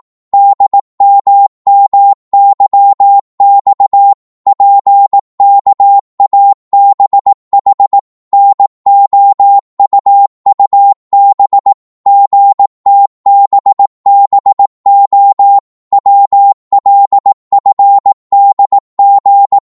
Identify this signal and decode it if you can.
Morse code, heard as a keyed tone, about 18 wpm, 800 Hz. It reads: DMMYX PKABH NOUUB GTBBO WLFDG